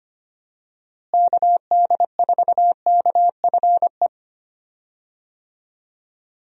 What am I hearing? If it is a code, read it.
Morse code, heard as a keyed tone, about 25 wpm, 700 Hz. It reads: KD4KFE
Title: USA Call Signs - 6 Characters - Repeated 3x - 25wpm